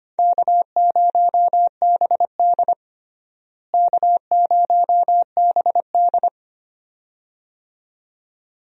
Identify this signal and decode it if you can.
Morse code, heard as a keyed tone, about 25 wpm, 700 Hz. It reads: K0BD K0BD
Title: USA Call Signs - 4 Characters - Repeated 2x - 25wpm